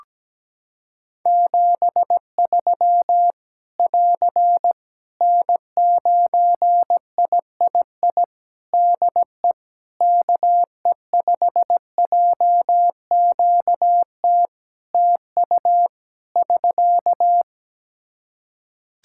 Morse code, heard as a keyed tone, about 17 wpm, 700 Hz. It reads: 73 <AR> N9III DE KE5JQT TU <SK>
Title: Practice QSO - Exchange 4 - Ending - 17wpm